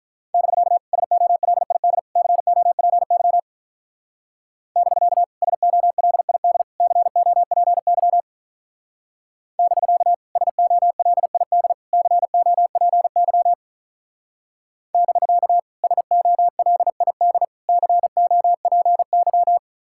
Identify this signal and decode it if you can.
Morse code, heard as a keyed tone, about 44 wpm, 700 Hz. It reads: <BK> SOLID COPY <BK> SOLID COPY <BK> SOLID COPY <BK> SOLID COPY